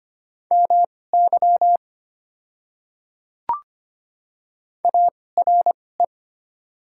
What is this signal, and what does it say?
Morse code, heard as a keyed tone, about 25 wpm, spaced as for 18 wpm, 700 Hz. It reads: MY ARE
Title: 2x Character Spacing - Top 500 Words - 25wpm